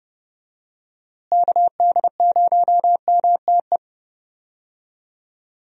Morse code, heard as a keyed tone, about 31 wpm, 700 Hz. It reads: KD0MTE
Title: USA Call Signs - 6 Characters - Repeated 2x - 30wpm